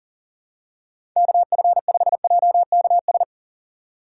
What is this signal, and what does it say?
Morse code, heard as a keyed tone, about 40 wpm, 700 Hz. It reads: KF5JKS
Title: USA Call Signs - 6 Characters - 6x Character Spacing - 40wpm